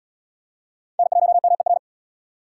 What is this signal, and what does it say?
Morse code, heard as a keyed tone, about 66 wpm, 700 Hz. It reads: N1MER